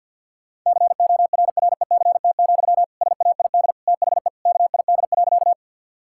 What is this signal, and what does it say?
Morse code, heard as a keyed tone, about 50 wpm, 700 Hz. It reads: CORRECT, SAID THE KID.
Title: Sentences from Top 1600 Words 50wpm